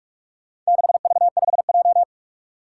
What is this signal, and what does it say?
Morse code, heard as a keyed tone, about 41 wpm, 700 Hz. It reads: 6V5J
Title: Four Challenging Letter-Number Combinations 45wpm